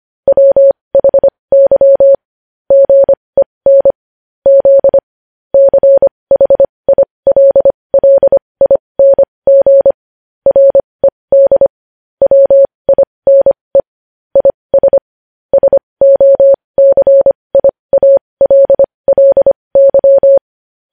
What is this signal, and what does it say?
Morse code, heard as a keyed tone, about 25 wpm, spaced as for 20 wpm, 550 Hz. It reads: WHY GEN Z CHILLING RED WINE IS SOCIALLY
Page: news-headlines-in-morse-code-at-20-wpm